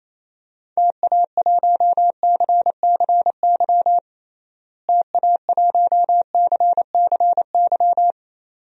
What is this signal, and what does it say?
Morse code, heard as a keyed tone, about 28 wpm, 700 Hz. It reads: TA1CCY TA1CCY
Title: DXCC-100 Call Signs - 6 Characters - Repeated 2x - 28wpm